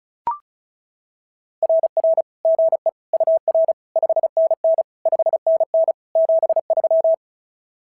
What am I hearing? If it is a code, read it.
Morse code, heard as a keyed tone, about 36 wpm, 660 Hz. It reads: RR GE UR 5NN 5NN 73
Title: SOTA QSO Practice 35wpm